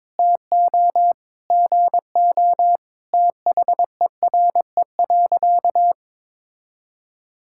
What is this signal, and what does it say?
Morse code, heard as a keyed tone, about 22 wpm, 700 Hz. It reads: TO GO THERE.
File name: practice / 22wpm / Sentences with 4-Word Phrases - 22wpm